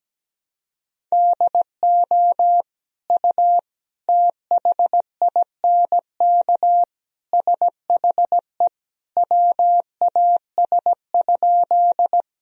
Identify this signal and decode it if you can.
Morse code, heard as a keyed tone, about 17 wpm, 700 Hz. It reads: DO U THINK SHE WAS?
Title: Sentences from Top 200 Words and Abbreviations 17wpm